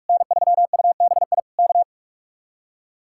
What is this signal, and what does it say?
Morse code, heard as a keyed tone, about 50 wpm, 700 Hz. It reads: N3UBI K